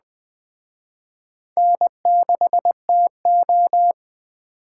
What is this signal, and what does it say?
Morse code, heard as a keyed tone, about 21 wpm, 700 Hz. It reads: N6TO
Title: Speed Racing - USA 1x2 Call Signs 17wpm